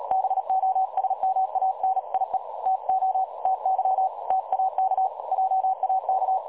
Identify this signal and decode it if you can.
Morse code, heard as a keyed tone, about 37 wpm, 760 Hz. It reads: BOSMANIE TO TY TAK JAK
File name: dlazaawansowanych / nagraniazpasma / websdr recording 3565.0kHz